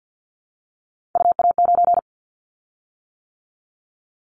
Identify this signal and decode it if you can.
Morse code, heard as a keyed tone, about 50 wpm, 700 Hz. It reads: UA9
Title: Callsign Prefixes for Top 50 DXCC - 50wpm